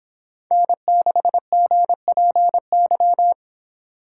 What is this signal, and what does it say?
Morse code, heard as a keyed tone, about 26 wpm, 700 Hz. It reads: N6GPY